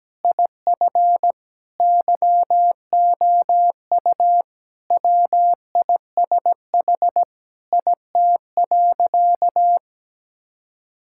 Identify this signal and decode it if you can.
Morse code, heard as a keyed tone, about 17 wpm, 700 Hz. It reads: IF YOU WISH IT.
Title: Sentences from Top 1600 Words 17wpm